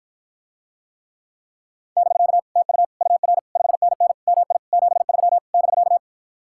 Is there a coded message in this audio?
Yes: Morse code from 2.0 to 6.0 s, heard as a keyed tone, about 52 wpm, 700 Hz. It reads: <BK> TU UR 5NN MI 73 <BK>